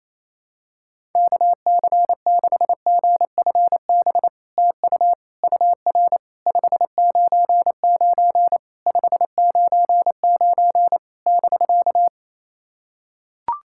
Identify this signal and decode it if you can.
Morse code, heard as a keyed tone, about 27 wpm, 700 Hz. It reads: KC6GFB TU UR 599 599 <BK>
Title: SOTA QSO Practice 28wpm